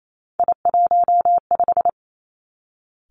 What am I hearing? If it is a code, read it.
Morse code, heard as a keyed tone, about 28 wpm, 700 Hz. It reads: I15
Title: Three Challenging Letter-Number Combinations 28wpm